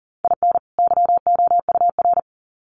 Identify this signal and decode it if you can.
Morse code, heard as a keyed tone, about 40 wpm, 700 Hz. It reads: IN YOUR
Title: Sets of 2 Words 40wpm